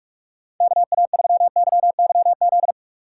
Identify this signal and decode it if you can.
Morse code, heard as a keyed tone, about 42 wpm, 690 Hz. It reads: KA3YYZ
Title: USA Call Signs - 6 Characters - 3x Character Spacing - 45wpm